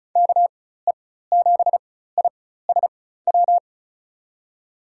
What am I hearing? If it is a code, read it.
Morse code, heard as a keyed tone, about 36 wpm, spaced as for 16 wpm, 700 Hz. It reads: KE7ISW
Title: USA Call Signs - 6 Characters - 4x Character Spacing - 35wpm